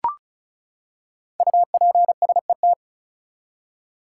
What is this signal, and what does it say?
Morse code, heard as a keyed tone, about 36 wpm, 700 Hz. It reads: UPSET
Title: Five Letter Words - 35wpm